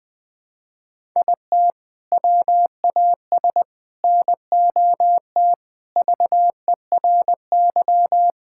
Morse code, heard as a keyed tone, about 21 wpm, 700 Hz. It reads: IT WAS NOT VERY
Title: Sentences from Top 1000 Words - Review Part 5 - 20wpm